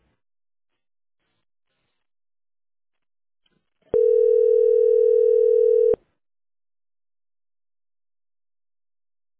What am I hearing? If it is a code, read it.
Ringback tone (North America), 440 + 480 Hz.